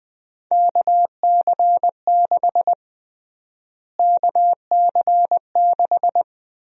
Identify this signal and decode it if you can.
Morse code, heard as a keyed tone, about 21 wpm, 700 Hz. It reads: KC6 KC6
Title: USA Call Sign Prefixes - Repeated 2x - 20wpm